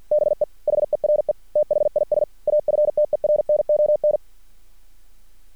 Morse code, heard as a keyed tone, about 50 wpm, 600 Hz. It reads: BE HERE THIS AFTERNON
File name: archived / CRSnet